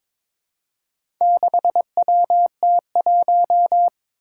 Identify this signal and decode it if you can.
Morse code, heard as a keyed tone, about 22 wpm, 700 Hz. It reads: 6WT1
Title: Four Challenging Letter-Number Combinations 22wpm